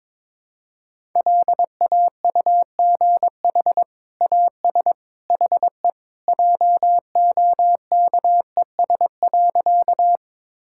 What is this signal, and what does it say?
Morse code, heard as a keyed tone, about 22 wpm, 700 Hz. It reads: LAUGH AS HE JOKES.
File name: practice / 22wpm / Sentences with 2-Word Phrases - 22wpm